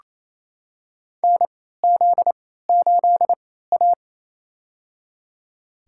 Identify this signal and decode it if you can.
Morse code, heard as a keyed tone, about 28 wpm, spaced as for 16 wpm, 700 Hz. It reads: NZ8A
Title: USA Call Signs - 4 Characters - 3x Character Spacing - 28wpm